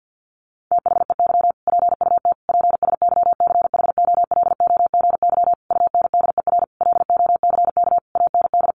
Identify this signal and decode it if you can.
Morse code, heard as a keyed tone, about 50 wpm, 700 Hz. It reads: THEY PUT PSYCHOLOGY UNDER LOCK AND
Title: Sentences from Top 1400 Words 50wpm